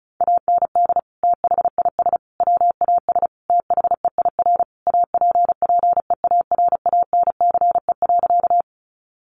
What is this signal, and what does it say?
Morse code, heard as a keyed tone, about 34 wpm, 700 Hz. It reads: AND THIS WAS THEIR APPEARANCE.
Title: Sentences from Top 1100 Words 35wpm